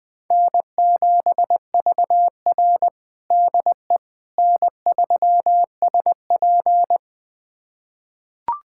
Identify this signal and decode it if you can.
Morse code, heard as a keyed tone, about 20 wpm, 700 Hz. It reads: N7VR DE N3SP
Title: USA Callsign DE Callsign - 4 Characters - 20wpm